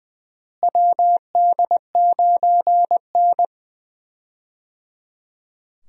Morse code, heard as a keyed tone, about 20 wpm, 700 Hz. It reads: WD9N
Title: USA Call Signs - 4 Characters - Repeated 3x - 20wpm